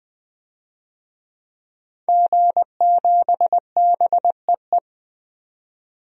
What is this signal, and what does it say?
Morse code, heard as a keyed tone, about 20 wpm, 700 Hz. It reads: G7BEE
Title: DXCC-100 Call Signs - 5 Characters - 20wpm